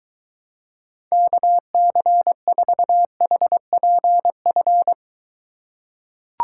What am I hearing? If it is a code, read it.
Morse code, heard as a keyed tone, about 21 wpm, 700 Hz. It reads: KC4HPF